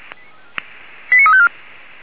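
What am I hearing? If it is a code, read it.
Five-tone selective call (ZVEI), heard as quick runs of five tones, tones of about 70 ms. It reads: 89354
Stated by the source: Five-tone ZVEI selective call 89346 received by SDR.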